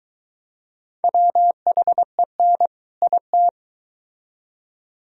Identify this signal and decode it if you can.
Morse code, heard as a keyed tone, about 24 wpm, 700 Hz. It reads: WHEN IT